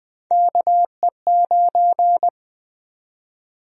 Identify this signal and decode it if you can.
Morse code, heard as a keyed tone, about 20 wpm, 700 Hz. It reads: KE9
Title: USA Call Sign Prefixes - 20wpm